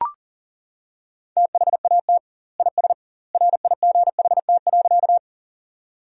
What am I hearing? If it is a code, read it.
Morse code, heard as a keyed tone, about 40 wpm, 700 Hz. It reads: THAT IS RIGHT.
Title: 2x Word-Spacing - Sentences from Top 200 Words - 40wpm